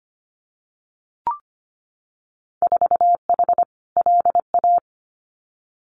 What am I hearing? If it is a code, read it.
Morse code, heard as a keyed tone, about 25 wpm, 700 Hz. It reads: 4H LA
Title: Winter Field Day Exchanges Encore 25wpm